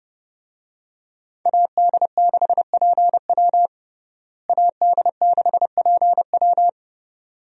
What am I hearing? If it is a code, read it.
Morse code, heard as a keyed tone, about 32 wpm, 700 Hz. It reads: AD6PW AD6PW
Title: USA Call Signs - 5 Characters - Repeated 2x - 30wpm